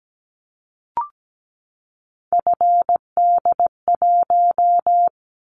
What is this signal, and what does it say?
Morse code, heard as a keyed tone, about 17 wpm, 700 Hz. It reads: FD1
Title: Callsign Prefixes for Top 50 DXCC - 17wpm